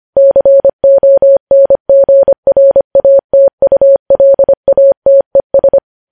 Morse code, heard as a keyed tone, about 25 wpm, 550 Hz. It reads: CONGRATULATES
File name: podcast / News headlines-1x-25x25